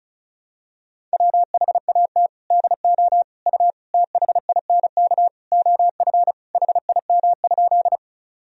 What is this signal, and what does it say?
Morse code, heard as a keyed tone, about 36 wpm, 700 Hz. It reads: WHAT DO U THINK OF HIM?
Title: Sentences from Top 200 Words and Abbreviations 35wpm